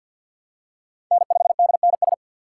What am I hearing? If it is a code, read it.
Morse code, heard as a keyed tone, about 51 wpm, 700 Hz. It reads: N5DNS
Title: USA Call Signs - 4x Character Spacing - 50wpm